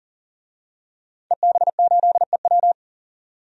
Morse code, heard as a keyed tone, about 40 wpm, 700 Hz. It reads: EB8EW
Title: DXCC-100 Call Signs - 5 Characters - 40wpm